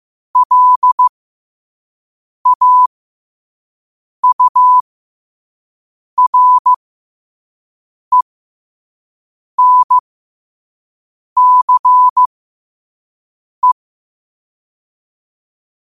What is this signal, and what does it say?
Morse code, heard as a keyed tone, about 15 wpm, spaced as for 5 wpm, 1000 Hz. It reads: LAURENCE